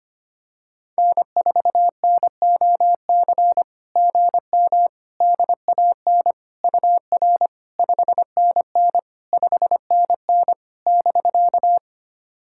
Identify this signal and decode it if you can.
Morse code, heard as a keyed tone, about 25 wpm, 700 Hz. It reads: N4NOC GM DAN UR 5NN 5NN <BK>
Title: SOTA QSO Practice 25wpm